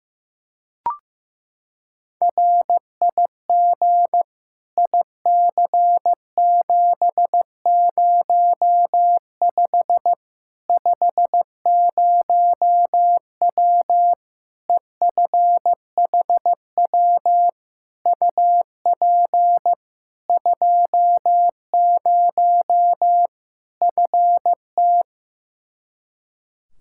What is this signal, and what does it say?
Morse code, heard as a keyed tone, about 15 wpm, 700 Hz. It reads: RIG IC705 50W EFHW UP 20 FT
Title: Practice QSO - Exchange 2 - Info - 15wpm